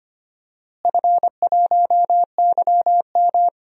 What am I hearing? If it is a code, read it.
Morse code, heard as a keyed tone, about 25 wpm, 700 Hz. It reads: F1YM
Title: DXCC-100 Call Signs - 4 Characters - Repeated 3x - 25wpm